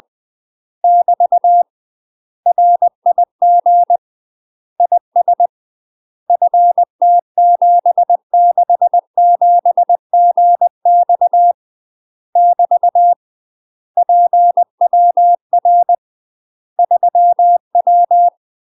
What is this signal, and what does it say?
Morse code, heard as a keyed tone, about 20 wpm, 700 Hz. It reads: <BT> RIG IS FT767GX <BT> PWR 3W
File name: UK MASTER QSOs TRAINING / 20WPM / UK MASTER QSOs 0067